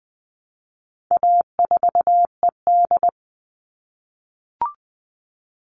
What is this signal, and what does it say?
Morse code, heard as a keyed tone, about 20 wpm, 700 Hz. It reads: A4ED
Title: All Cut Numbers - 4-Digit 20wpm